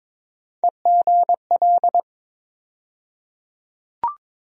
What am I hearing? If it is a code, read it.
Morse code, heard as a keyed tone, about 21 wpm, 700 Hz. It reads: EGL
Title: USA Call Sign Postfixes - 3 Letters 22wpm